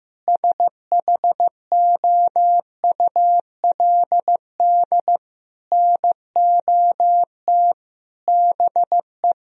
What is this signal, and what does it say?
Morse code, heard as a keyed tone, about 14 wpm, 700 Hz. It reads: SHOULD NOT BE
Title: Sentences from Top 1000 Words - Review Part 4 - 15wpm